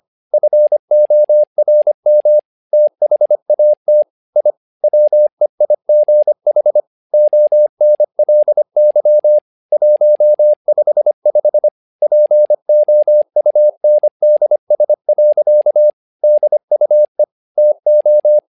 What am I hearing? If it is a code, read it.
Morse code, heard as a keyed tone, about 25 wpm, 600 Hz. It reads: FORM THAT I WEIGH ONLY 155 POUNDS. DUE TO
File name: learning center / tower trouble / tower trouble 25wpm